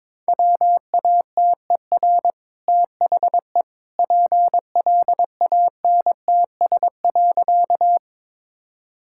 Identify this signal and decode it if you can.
Morse code, heard as a keyed tone, about 22 wpm, 690 Hz. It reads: WATER THE PLANTS.
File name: practice / 22wpm / Easy Sentences with Hints - 22wpm